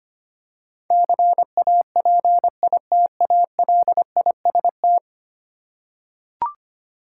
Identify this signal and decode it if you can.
Morse code, heard as a keyed tone, about 25 wpm, 700 Hz. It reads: CAPITALIST
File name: 10 Letter Words - First 3 Letters Given - 25wpm